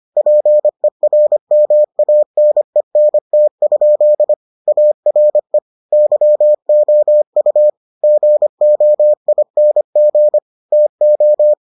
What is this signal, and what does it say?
Morse code, heard as a keyed tone, about 25 wpm, 600 Hz. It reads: PERMANENT? ARE YOU GOING TO